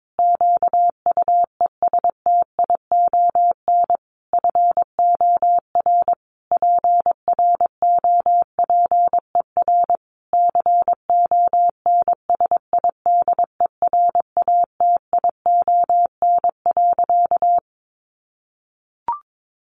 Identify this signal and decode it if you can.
Morse code, heard as a keyed tone, about 22 wpm, 700 Hz. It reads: QUESTION FOR PROPER CONSIDERATION.
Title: Sentences from Top 1500 Words 22wpm